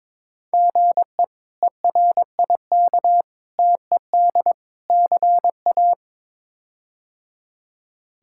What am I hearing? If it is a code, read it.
Morse code, heard as a keyed tone, about 22 wpm, 700 Hz. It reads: GE ERIK TED CA
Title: K1USN SST QSO Practice - 22wpm